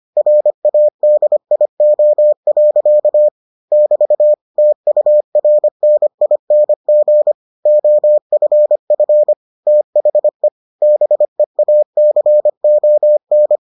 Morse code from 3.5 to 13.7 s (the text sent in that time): <BT> TURNING OFF THE BEACON